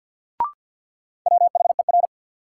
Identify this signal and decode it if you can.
Morse code, heard as a keyed tone, about 48 wpm, 700 Hz. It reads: WHER
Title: Sentences from Top 1500 Words - New Word Drill Alt - 50wpm